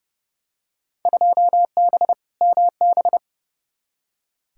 Morse code, heard as a keyed tone, about 29 wpm, 700 Hz. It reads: 2B MB
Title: Field Day Exchanges Encore - 30wpm